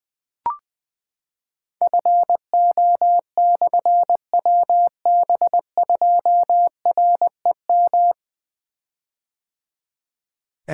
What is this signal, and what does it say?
Morse code, heard as a keyed tone, about 20 wpm, 700 Hz. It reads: FO/WB2REM
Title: Difficult Call Signs 20wpm